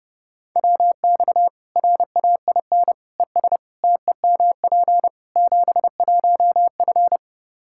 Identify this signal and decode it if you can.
Morse code, heard as a keyed tone, about 30 wpm, 700 Hz. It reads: WX RAIN ES TEMP 71F
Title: K1USN SST QSO Practice 30wpm